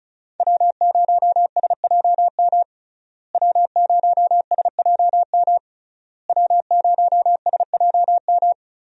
Morse code, heard as a keyed tone, about 34 wpm, 700 Hz. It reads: W0SJM W0SJM W0SJM
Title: USA Call Signs - 5 Characters - Repeated 3x - 35wpm